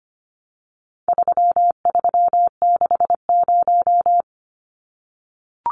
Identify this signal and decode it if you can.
Morse code, heard as a keyed tone, about 25 wpm, 700 Hz. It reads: 3360
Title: Numbers - 4-Digit 25wpm